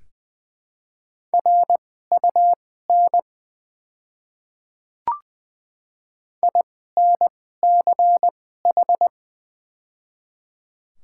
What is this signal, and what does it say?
Morse code, heard as a keyed tone, about 20 wpm, spaced as for 14 wpm, 700 Hz. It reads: RUN INCH
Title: 2x Character Spacing - Top 500 Words - 20wpm